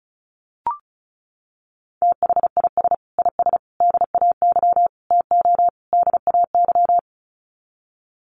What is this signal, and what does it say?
Morse code, heard as a keyed tone, about 34 wpm, 700 Hz. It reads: THIS IS DAY TO DAY
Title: Sets of 5 Words Encore 35wpm